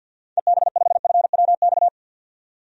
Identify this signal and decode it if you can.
Morse code, heard as a keyed tone, about 51 wpm, 700 Hz. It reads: EB5FWX
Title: DXCC-100 Call Signs - 6 Characters - 50wpm